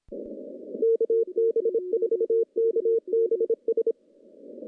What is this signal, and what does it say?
Morse code, heard as a keyed tone, about 25 wpm, 460 Hz. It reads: KB4KBS
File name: contests / CW-SS-k5tr- -0200